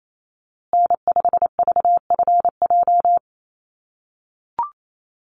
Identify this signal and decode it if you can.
Morse code, heard as a keyed tone, about 28 wpm, 700 Hz. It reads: N5VFJ
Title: USA Call Signs Repeated 3x 28wpm